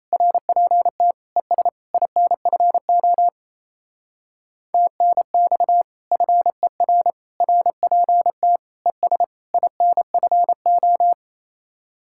Morse code, heard as a keyed tone, about 35 wpm, 700 Hz. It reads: RPT ES INFO TNX FER RPT ES INFO
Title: Speed-Racing Common QSO Phrases - 28wpm